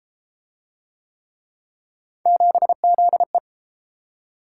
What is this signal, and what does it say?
Morse code, heard as a keyed tone, about 30 wpm, 700 Hz. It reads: 7ZE